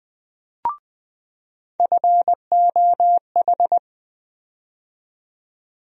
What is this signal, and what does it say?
Morse code, heard as a keyed tone, about 20 wpm, 700 Hz. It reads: FOH
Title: USA Call Sign Postfixes - 3 Letters 20wpm